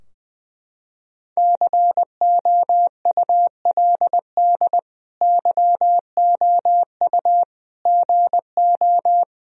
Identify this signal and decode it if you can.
Morse code, heard as a keyed tone, about 21 wpm, 700 Hz. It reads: COULD YOU GO
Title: Sentences from Top 1000 Words - Review Part 4 - 20wpm